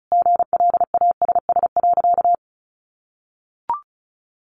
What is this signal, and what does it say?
Morse code, heard as a keyed tone, about 34 wpm, 700 Hz. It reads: GLASS.
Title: Sentences from Top 1300 Words 35wpm